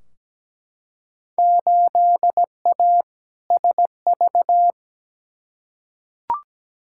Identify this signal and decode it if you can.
Morse code, heard as a keyed tone, about 17 wpm, 700 Hz. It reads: 8A SV
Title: Field Day Exchanges Encore - 17wpm